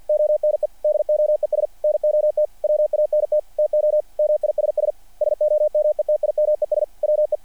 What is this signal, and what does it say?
Morse code, heard as a keyed tone, about 48 wpm, 600 Hz. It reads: ONE DOES NOT WANT TO MISS SOMETIMES WE
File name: archived / CRSnet